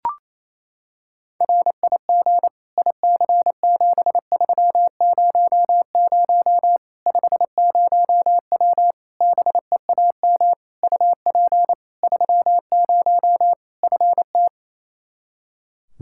Morse code, heard as a keyed tone, about 28 wpm, 700 Hz. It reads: RIG IC7300 50W BEAM UP 30 FT
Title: Practice QSO - Exchange 2 - Info - 28wpm